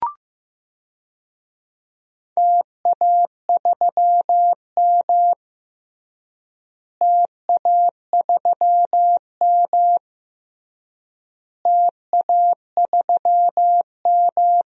Morse code, heard as a keyed tone, about 15 wpm, 700 Hz. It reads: TA3M TA3M TA3M
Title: DXCC-100 Call Signs - 4 Characters - Repeated 3x - 15wpm